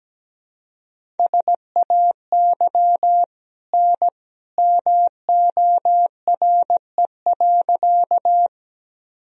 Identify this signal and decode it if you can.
Morse code, heard as a keyed tone, about 17 wpm, 700 Hz. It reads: SAY N MORE.
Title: Sentences from Top 200 Words and Abbreviations 17wpm